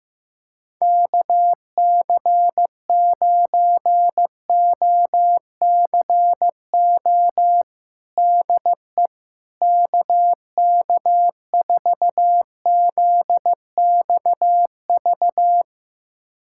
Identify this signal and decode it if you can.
Morse code, heard as a keyed tone, about 15 wpm, 700 Hz. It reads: KC9OCO DE KK4ZXV
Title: USA Callsign DE Callsign 15wpm